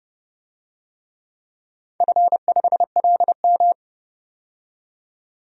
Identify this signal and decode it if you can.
Morse code, heard as a keyed tone, about 32 wpm, 700 Hz. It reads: F5LM
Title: DXCC-100 Call Signs - 4 Characters - 30wpm